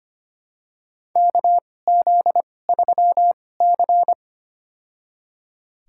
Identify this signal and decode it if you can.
Morse code, heard as a keyed tone, about 25 wpm, spaced as for 18 wpm, 700 Hz. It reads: KZ3C
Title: USA Call Signs - 2x Character Spacing - 25wpm